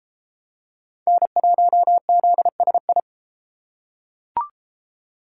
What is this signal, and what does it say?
Morse code, heard as a keyed tone, about 30 wpm, 700 Hz. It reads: N1ZSI